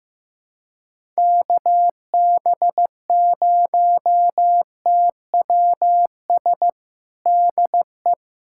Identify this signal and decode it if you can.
Morse code, heard as a keyed tone, about 15 wpm, 700 Hz. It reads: KB0TWS DE
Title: USA Callsign DE Callsign - 6 Characters - 15wpm